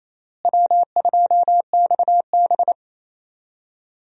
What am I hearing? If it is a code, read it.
Morse code, heard as a keyed tone, about 28 wpm, 700 Hz. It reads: W2XB
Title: USA Call Signs - 4 Characters - Repeated 3x - 28wpm